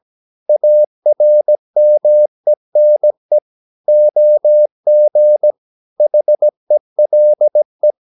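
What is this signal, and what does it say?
Morse code, heard as a keyed tone, about 17 wpm, 600 Hz. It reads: ARMENE OG HELE
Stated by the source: Slow (QRS) Morse practice at 17 wpm, from CW med Gnister.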